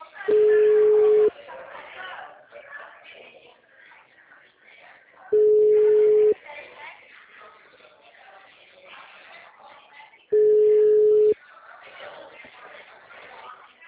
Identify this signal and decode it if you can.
Ringback tone (Europe, 425 Hz), with speech.